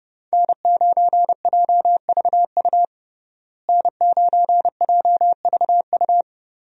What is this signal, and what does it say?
Morse code, heard as a keyed tone, about 30 wpm, 700 Hz. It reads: N9JVU N9JVU
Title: USA Call Signs - 5 Characters - Repeated 2x - 30wpm